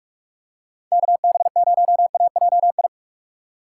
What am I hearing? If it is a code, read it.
Morse code, heard as a keyed tone, about 44 wpm, 700 Hz. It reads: KB0AJI
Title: USA Call Signs - 6 Characters - 6x Character Spacing - 45wpm